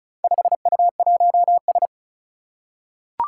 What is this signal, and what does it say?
Morse code, heard as a keyed tone, about 36 wpm, 700 Hz. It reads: 5U1S
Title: Four Challenging Letter-Number Combinations 35wpm